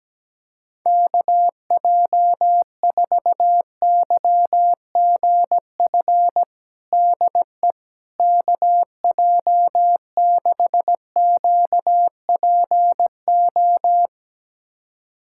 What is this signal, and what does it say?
Morse code, heard as a keyed tone, about 17 wpm, 700 Hz. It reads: KJ4YGF DE KJ6QPO
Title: USA Callsign DE Callsign 17wpm